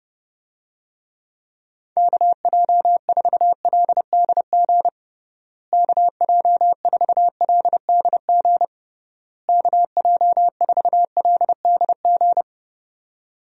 Morse code, heard as a keyed tone, about 30 wpm, 700 Hz. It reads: KJ4LDG KJ4LDG KJ4LDG
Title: USA Call Signs - 6 Characters - Repeated 3x - 30wpm